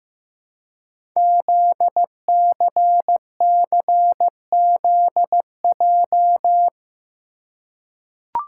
Morse code, heard as a keyed tone, about 15 wpm, 700 Hz. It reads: ZCCZJ
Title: Code Groups - 5 Letters - 15wpm